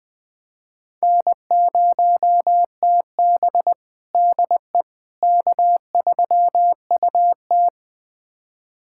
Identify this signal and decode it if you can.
Morse code, heard as a keyed tone, about 21 wpm, 700 Hz. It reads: N0TB DE K3UT
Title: USA Callsign DE Callsign - 4 Characters - 20wpm